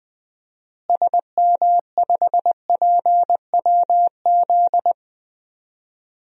Morse code, heard as a keyed tone, about 20 wpm, 700 Hz. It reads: SM5PWZ
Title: DXCC-100 Call Signs - 6 Characters - Repeated 3x - 20wpm